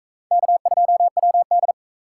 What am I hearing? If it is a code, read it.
Morse code, heard as a keyed tone, about 44 wpm, 700 Hz. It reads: K2WD